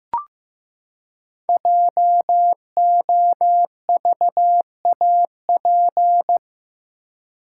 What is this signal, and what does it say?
Morse code, heard as a keyed tone, about 15 wpm, 700 Hz. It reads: JOVAP
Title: Code Groups - 5 Letters - 15wpm